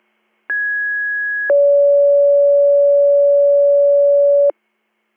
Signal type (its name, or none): two-tone page